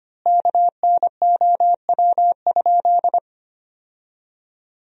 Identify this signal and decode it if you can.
Morse code, heard as a keyed tone, about 25 wpm, 700 Hz. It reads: KNOW?
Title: 3x Word-Spacing - Sentences from Top 200 Words - 25wpm